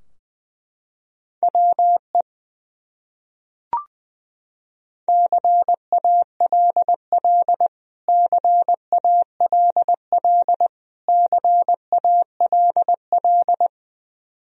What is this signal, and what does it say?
Morse code, heard as a keyed tone, about 20 wpm, 700 Hz. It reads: WE CALL CALL CALL
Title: Top 100 Words Repeated 20wpm